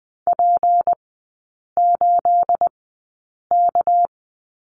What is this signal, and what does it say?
Morse code, heard as a keyed tone, about 20 wpm, spaced as for 8 wpm, 690 Hz. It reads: P8K
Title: 212 Lesson 34e - Number 8 - Three Characters 20wpm